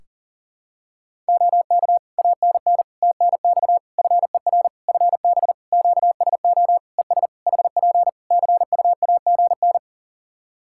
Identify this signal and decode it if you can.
Morse code, heard as a keyed tone, about 37 wpm, 700 Hz. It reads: OK ANN TNX FER FB QSO ES HP CUAGN
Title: Practice QSO - Exchange 4 - Ending - 40wpm